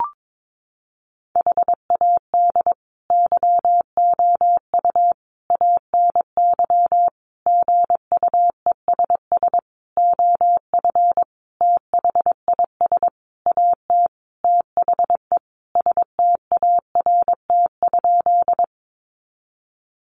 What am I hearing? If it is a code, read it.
Morse code, heard as a keyed tone, about 22 wpm, 700 Hz. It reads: HAD YOU ANY GUESS OF THIS AT THE START?
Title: Sentences from Top 1500 Words 22wpm